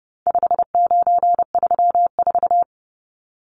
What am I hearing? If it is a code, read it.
Morse code, heard as a keyed tone, about 30 wpm, 700 Hz. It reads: 5934
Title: Numbers - 4-Digit 30wpm